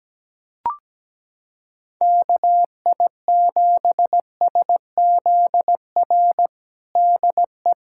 Morse code, heard as a keyed tone, about 17 wpm, 700 Hz. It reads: KI7SZR DE
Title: USA Callsign DE Callsign - 6 Characters - 17wpm